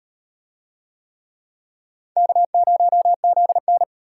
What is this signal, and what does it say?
Morse code, heard as a keyed tone, about 36 wpm, 700 Hz. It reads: K0ZN